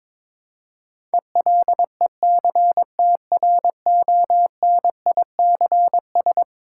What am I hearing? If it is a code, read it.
Morse code, heard as a keyed tone, about 22 wpm, 700 Hz. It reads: ELECTRONICS
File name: practice / 22wpm / Eleven Letter Words - 22wpm